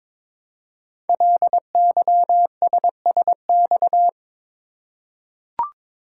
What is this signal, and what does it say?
Morse code, heard as a keyed tone, about 22 wpm, 700 Hz. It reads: LYSSX
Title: Code Groups - 5 Letters - 22wpm